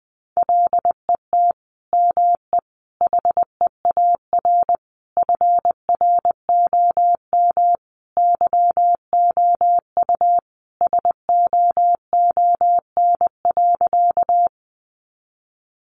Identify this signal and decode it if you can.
Morse code, heard as a keyed tone, about 20 wpm, 700 Hz. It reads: LET ME HEAR FROM YOU SOON.
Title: Sentences from Top 400 Words 20wpm